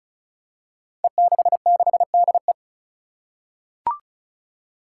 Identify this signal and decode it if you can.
Morse code, heard as a keyed tone, about 36 wpm, 700 Hz. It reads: E66DE
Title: Five Challenging Letter-Number Combinations 35wpm